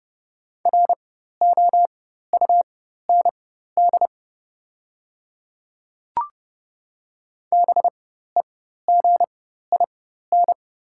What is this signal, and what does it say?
Morse code, heard as a keyed tone, about 29 wpm, spaced as for 14 wpm, 690 Hz. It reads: ROUND BEGIN
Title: 4x Character Spacing - Top 500 Words - 30wpm